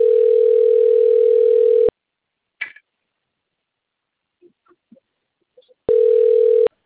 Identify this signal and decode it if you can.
Ringback tone (North America), 440 + 480 Hz.